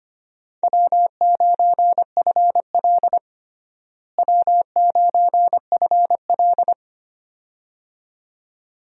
Morse code, heard as a keyed tone, about 25 wpm, 700 Hz. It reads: W9FL W9FL
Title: USA Call Signs - 4 Characters - Repeated 2x - 25wpm